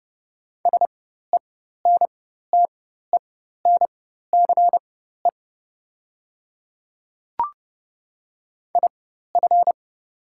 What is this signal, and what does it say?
Morse code, heard as a keyed tone, about 30 wpm, spaced as for 14 wpm, 700 Hz. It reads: SENTENCE IF
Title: 4x Character Spacing - Top 300 Words - 30wpm